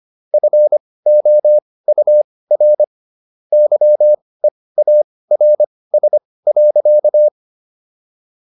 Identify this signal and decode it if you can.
Morse code, heard as a keyed tone, about 25 wpm, spaced as for 18 wpm, 600 Hz. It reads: FOUR YEARS.